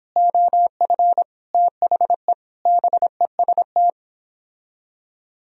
Morse code, heard as a keyed tone, about 29 wpm, 700 Hz. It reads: OF THE BEST